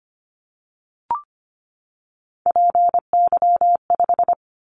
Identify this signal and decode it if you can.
Morse code, heard as a keyed tone, about 25 wpm, 700 Hz. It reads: PY5
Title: Callsign Prefixes for Top 100 DXCC - 25wpm